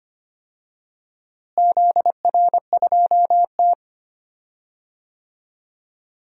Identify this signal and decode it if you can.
Morse code, heard as a keyed tone, about 25 wpm, 700 Hz. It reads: ZR2T
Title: DXCC-100 Call Signs - 4 Characters - Repeated 3x - 25wpm